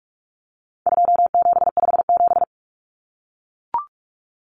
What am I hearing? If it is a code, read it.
Morse code, heard as a keyed tone, about 44 wpm, 700 Hz. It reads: 2757
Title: Common Cut Numbers - 4-Digit 45wpm